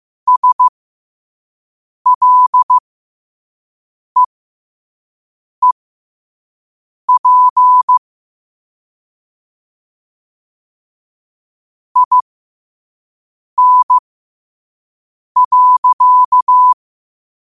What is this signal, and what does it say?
Morse code, heard as a keyed tone, about 15 wpm, spaced as for 5 wpm, 1000 Hz. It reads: SLEEP IN.
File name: qotdfiles / 5wpm